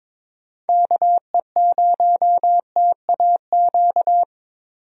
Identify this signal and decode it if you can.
Morse code, heard as a keyed tone, about 22 wpm, 700 Hz. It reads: KE0TAQ
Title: USA Call Signs - 6 Characters - Repeated 2x - 22wpm